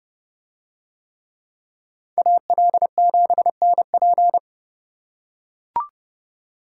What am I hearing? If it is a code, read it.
Morse code, heard as a keyed tone, about 30 wpm, 700 Hz. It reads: AL7NP